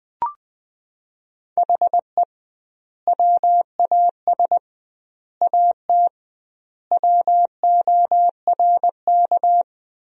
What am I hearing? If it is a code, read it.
Morse code, heard as a keyed tone, about 20 wpm, 700 Hz. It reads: HE WAS AT WORK
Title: 2x Word-Spacing - Sentences from Top 200 Words - 20wpm